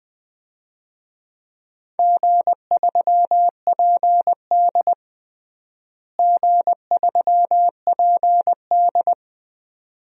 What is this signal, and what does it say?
Morse code, heard as a keyed tone, about 20 wpm, 700 Hz. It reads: G3PD G3PD
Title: DXCC-100 Call Signs - 4 Characters - Repeated 2x - 20wpm